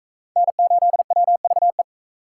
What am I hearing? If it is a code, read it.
Morse code, heard as a keyed tone, about 38 wpm, 700 Hz. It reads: N8WVE